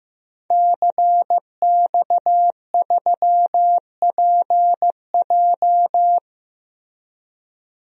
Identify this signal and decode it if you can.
Morse code, heard as a keyed tone, about 16 wpm, 700 Hz. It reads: CX3PJ
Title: DXCC-100 Call Signs - 5 Characters - Repeated 3x - 15wpm